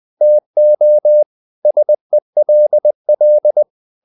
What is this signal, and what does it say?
Morse code, heard as a keyed tone, about 20 wpm, 600 Hz. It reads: TO SELL
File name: the old man and the sea part 2 20wpm